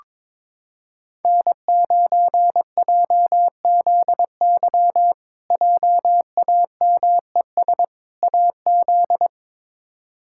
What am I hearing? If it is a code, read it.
Morse code, heard as a keyed tone, about 22 wpm, 700 Hz. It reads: N9JZY JAMES AZ
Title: K1USN SST QSO Practice - 22wpm